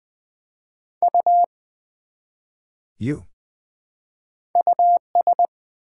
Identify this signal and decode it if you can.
Morse code, heard as a keyed tone, about 20 wpm, 700 Hz. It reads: U US
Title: Build A Word - 1 Letter At A Time - 3-Letter Words - 20wpm